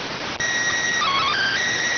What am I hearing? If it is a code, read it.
SSTV (slow-scan television), PD 290.